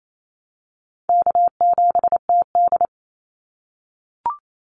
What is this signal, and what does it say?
Morse code, heard as a keyed tone, about 28 wpm, 700 Hz. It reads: K7TD
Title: Call Signs - 28wpm